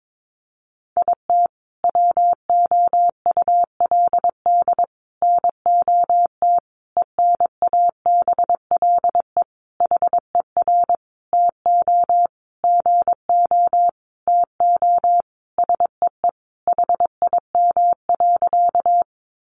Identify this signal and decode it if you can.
Morse code, heard as a keyed tone, about 22 wpm, 700 Hz. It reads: IT WOULD NOT ENABLE HER TO GO TO SEE HIM.
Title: Sentences from Top 1500 Words 22wpm